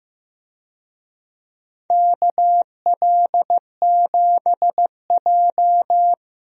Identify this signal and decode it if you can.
Morse code, heard as a keyed tone, about 15 wpm, 700 Hz. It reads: KL7J
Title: DXCC-100 Call Signs - 4 Characters - Repeated 3x - 15wpm